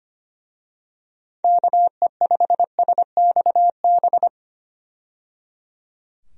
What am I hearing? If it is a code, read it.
Morse code, heard as a keyed tone, about 25 wpm, 700 Hz. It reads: KE5SXB
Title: USA Call Signs - 6 Characters - 25wpm